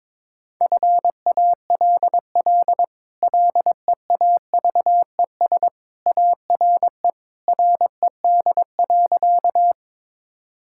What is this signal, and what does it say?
Morse code, heard as a keyed tone, about 22 wpm, 700 Hz. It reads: FALL LEAVES ARE RED.
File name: practice / 22wpm / Easy Sentences with Hints - 22wpm